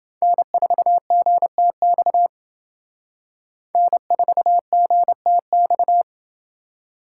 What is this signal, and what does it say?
Morse code, heard as a keyed tone, about 28 wpm, 700 Hz. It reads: N4GTX N4GTX